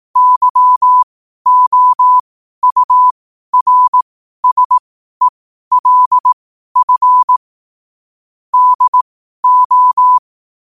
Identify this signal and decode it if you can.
Morse code, heard as a keyed tone, about 18 wpm, spaced as for 12 wpm, 1000 Hz. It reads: YOURSELF DO
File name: qotdfiles / 12wpm